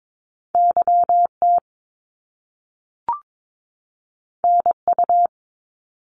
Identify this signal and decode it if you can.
Morse code, heard as a keyed tone, about 22 wpm, 700 Hz. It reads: YT NU
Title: Canadian Provinces and Territories 22wpm